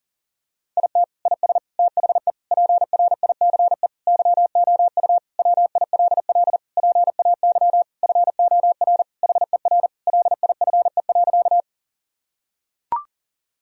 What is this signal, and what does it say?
Morse code, heard as a keyed tone, about 37 wpm, 700 Hz. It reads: IT IS THE PRICE YOU WILL PAY FOR HER LIFE.
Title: Sentences from Top 1000 Words - Review Part 4 - 40wpm